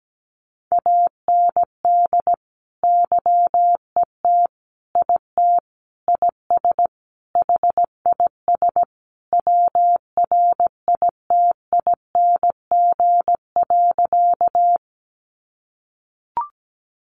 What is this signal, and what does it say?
Morse code, heard as a keyed tone, about 17 wpm, 700 Hz. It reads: AND YET IT IS HIS WRITING.
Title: Sentences from Top 800 Words 17wpm